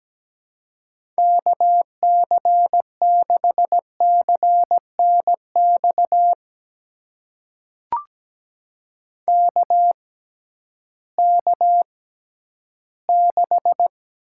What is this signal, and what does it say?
Morse code, heard as a keyed tone, about 17 wpm, spaced as for 11 wpm, 700 Hz. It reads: KC6CNX KK6
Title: USA Call Signs - 6 Characters - 6x Character Spacing - 17wpm